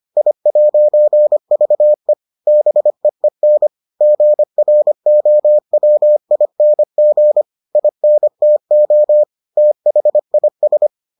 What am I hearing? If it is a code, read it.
Morse code, heard as a keyed tone, about 25 wpm, 600 Hz. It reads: I'VE BEEN GROWING INTO THIS